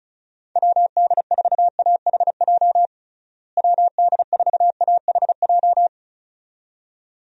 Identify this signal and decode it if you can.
Morse code, heard as a keyed tone, about 34 wpm, 700 Hz. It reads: WD4AHJ WD4AHJ
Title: USA Call Signs - 6 Characters - Repeated 2x - 35wpm